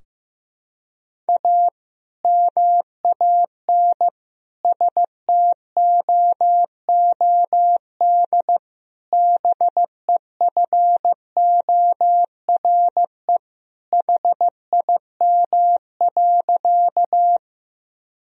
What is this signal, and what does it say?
Morse code, heard as a keyed tone, about 15 wpm, 700 Hz. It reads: A MAN STOOD BEFORE HIM.
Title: Sentences from Top 1000 Words - Review Part 2 - 15wpm